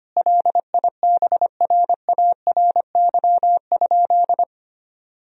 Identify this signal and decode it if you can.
Morse code, heard as a keyed tone, about 25 wpm, 700 Hz. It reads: LIBRARY?
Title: Sentences from Top 1000 Words - Review Part 4 - 25wpm